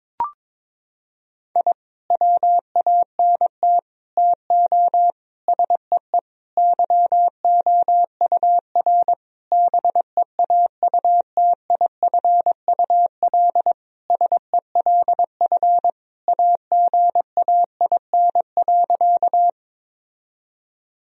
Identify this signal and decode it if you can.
Morse code, heard as a keyed tone, about 22 wpm, 700 Hz. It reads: I WANT TO SEE YOUR BEAUTIFUL SELF AGAIN.
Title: Sentences from Top 1000 Words - Review Part 1 - 22wpm